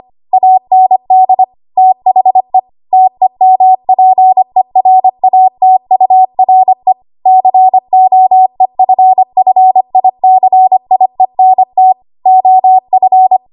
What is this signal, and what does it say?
Morse code, heard as a keyed tone, about 25 wpm, 750 Hz. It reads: AND THE TEMPERATURE COEFFICIENT OF
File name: Archive / 25 WPM / 25WPM